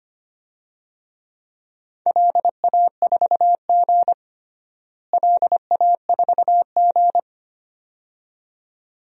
Morse code, heard as a keyed tone, about 25 wpm, 700 Hz. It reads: LA4G LA4G
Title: DXCC-100 Call Signs - 4 Characters - Repeated 2x - 25wpm